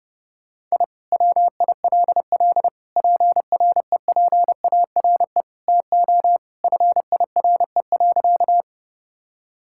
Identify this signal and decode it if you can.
Morse code, heard as a keyed tone, about 29 wpm, 700 Hz. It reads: I WILL PREPARE TO FIRE.
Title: Sentences from Top 1000 Words - Review Part 6 - 30wpm